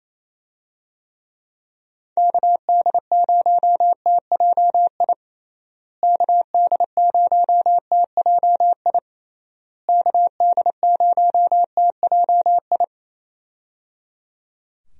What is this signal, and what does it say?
Morse code, heard as a keyed tone, about 28 wpm, 700 Hz. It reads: KD0TJI KD0TJI KD0TJI
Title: USA Call Signs - 6 Characters - Repeated 3x - 28wpm